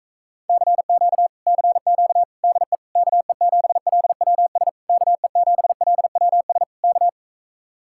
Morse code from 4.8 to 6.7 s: KE7LWS

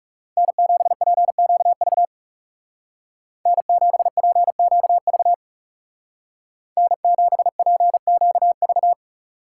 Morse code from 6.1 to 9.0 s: N7PQV

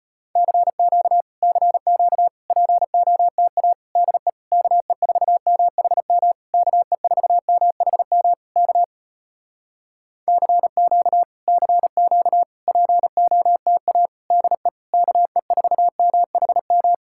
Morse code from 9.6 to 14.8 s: CQ CQ POTA DE

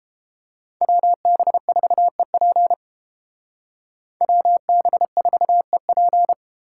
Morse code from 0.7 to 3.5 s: WB4EP

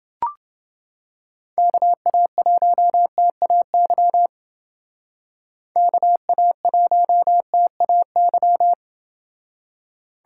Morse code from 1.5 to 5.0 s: KA1TAY

